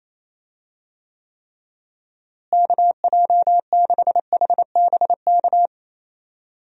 Morse code, heard as a keyed tone, about 28 wpm, 700 Hz. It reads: KJ6HBK
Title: USA Call Signs - 6 Characters - Repeated 3x - 28wpm